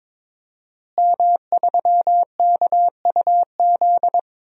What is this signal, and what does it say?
Morse code, heard as a keyed tone, about 22 wpm, 700 Hz. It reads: M3KUZ
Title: DXCC-100 Call Signs - 5 Characters - Repeated 2x - 22wpm